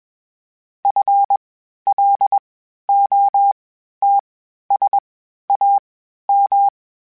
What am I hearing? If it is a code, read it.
Morse code, heard as a keyed tone, about 21 wpm, spaced as for 12 wpm, 800 Hz. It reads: FLOTSAM
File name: Misc / Misc-mp3s / WordPairs sp @20